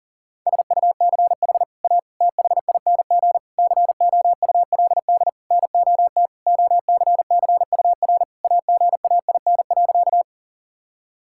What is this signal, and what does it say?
Morse code, heard as a keyed tone, about 40 wpm, 700 Hz. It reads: SUCH A THING COULD NOT OCCUR AGAIN.
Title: Sentences from Top 1000 Words - Review Part 6 - 40wpm